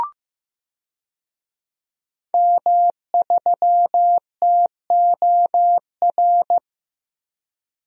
Morse code, heard as a keyed tone, about 15 wpm, 700 Hz. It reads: M3TOR
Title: DXCC-100 Call Signs - 5 Characters - Repeated 3x - 15wpm